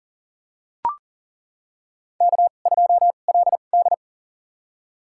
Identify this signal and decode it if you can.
Morse code, heard as a keyed tone, about 40 wpm, spaced as for 29 wpm, 690 Hz. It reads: K2LD
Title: USA Call Signs - 4 Characters - 2x Character Spacing - 40wpm